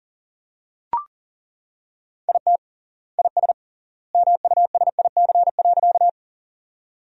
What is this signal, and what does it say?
Morse code, heard as a keyed tone, about 40 wpm, 700 Hz. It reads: IT IS MUSIC.
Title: 3x Word-Spacing - Sentences from Top 300 Words - 40wpm